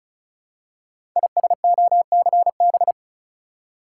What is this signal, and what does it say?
Morse code, heard as a keyed tone, about 36 wpm, 690 Hz. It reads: ISOCB
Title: Code Groups - 5 Letters - 35wpm